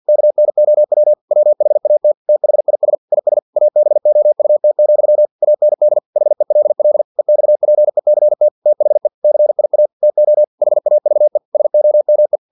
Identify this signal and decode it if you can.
Morse code, heard as a keyed tone, about 50 wpm, 600 Hz. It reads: KNOW WHAT THIS IS ABOUT, AND HELL EXPECT THE CIA TO HAVE SOME
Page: Never 49wpm